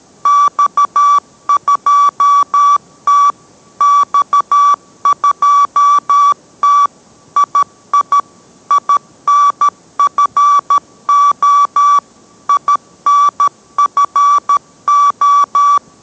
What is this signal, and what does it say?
Morse code, heard as a keyed tone, about 13 wpm, 1150 Hz. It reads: X2T X2T II INFO INFO